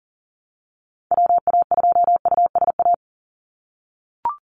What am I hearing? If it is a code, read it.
Morse code, heard as a keyed tone, about 39 wpm, 700 Hz. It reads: WA2USA
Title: NA SOTA Activator Callsigns 40wpm